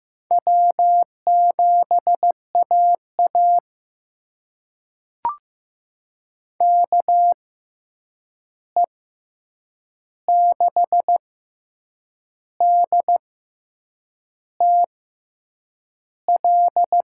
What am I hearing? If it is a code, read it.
Morse code, heard as a keyed tone, about 15 wpm, spaced as for 7 wpm, 700 Hz. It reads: W7AA KE6DTL
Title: USA Call Signs - 6x Character Spacing - 15wpm